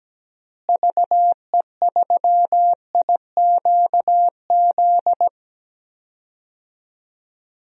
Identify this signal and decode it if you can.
Morse code, heard as a keyed tone, about 17 wpm, 700 Hz. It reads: VE3IQZ
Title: DXCC-100 Call Signs - 6 Characters - Repeated 3x - 17wpm